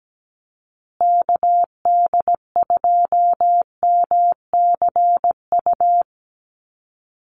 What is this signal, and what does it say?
Morse code, heard as a keyed tone, about 17 wpm, 700 Hz. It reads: KD2MCU
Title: USA Call Signs Repeated 2x 17wpm